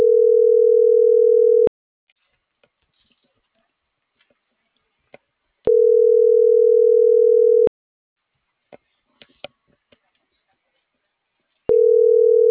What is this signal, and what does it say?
Ringback tone (North America), 440 + 480 Hz.